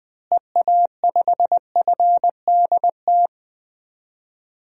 Morse code, heard as a keyed tone, about 20 wpm, 700 Hz. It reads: EA5FDT
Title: DXCC-100 Call Signs - 6 Characters - Repeated 2x - 20wpm